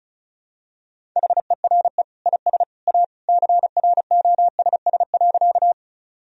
Morse code, heard as a keyed tone, about 36 wpm, 700 Hz. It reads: HERE IS A CROSS.
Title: Sentences from Top 1000 Words - Review Part 1 - 35wpm